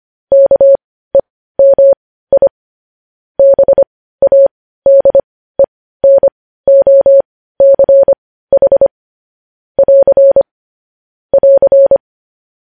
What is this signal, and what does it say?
Morse code, heard as a keyed tone, about 25 wpm, spaced as for 15 wpm, 550 Hz. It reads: KEMI BADENOCH <AR> <AR>
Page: news-headlines-in-morse-code-at-15-wpm